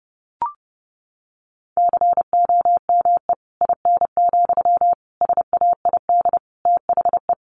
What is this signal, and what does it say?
Morse code, heard as a keyed tone, about 30 wpm, 700 Hz. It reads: COME IN, SAID THE
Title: Sentences from Top 1000 Words 30wpm